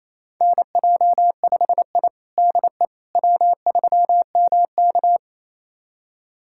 Morse code, heard as a keyed tone, about 27 wpm, 700 Hz. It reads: NJ5I DE W3MK
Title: USA Callsign DE Callsign - 4 Characters - 28wpm